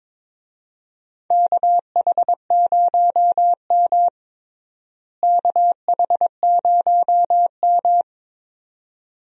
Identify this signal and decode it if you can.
Morse code, heard as a keyed tone, about 22 wpm, 700 Hz. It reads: KH0M KH0M
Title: USA Call Signs - 4 Characters - Repeated 2x - 22wpm